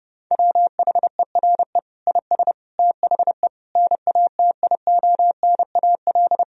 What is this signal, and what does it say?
Morse code, heard as a keyed tone, about 32 wpm, 700 Hz. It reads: WHERE IS THE NATIONAL
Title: Sentences from Top 1000 Words - Review Part 4 - 30wpm